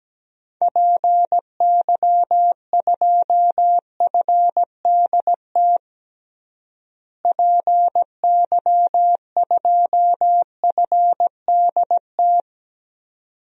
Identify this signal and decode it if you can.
Morse code, heard as a keyed tone, about 17 wpm, 700 Hz. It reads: PY2FDT PY2FDT
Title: DXCC-100 Call Signs - 6 Characters - Repeated 2x - 17wpm